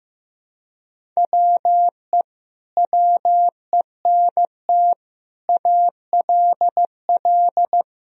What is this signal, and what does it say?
Morse code, heard as a keyed tone, about 16 wpm, 700 Hz. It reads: WE WENT ALL
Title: Sentences from Top 1000 Words - Review Part 1 - 15wpm